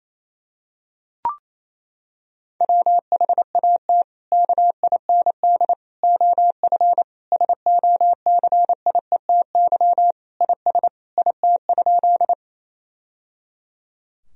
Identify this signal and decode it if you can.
Morse code, heard as a keyed tone, about 28 wpm, 700 Hz. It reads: WHAT KIND OF SOCIETY IS IT?
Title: Sentences from Top 1000 Words - Review Part 4 - 28wpm